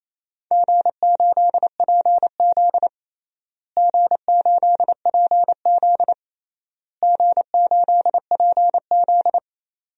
Morse code, heard as a keyed tone, about 28 wpm, 700 Hz. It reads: G8PZ G8PZ G8PZ
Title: DXCC-100 Call Signs - 4 Characters - Repeated 3x - 28wpm